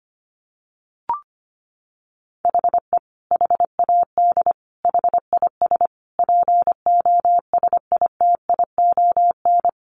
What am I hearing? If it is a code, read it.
Morse code, heard as a keyed tone, about 25 wpm, 700 Hz. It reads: HE HAD HIS POSITION
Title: Sentences from Top 900 Words 25wpm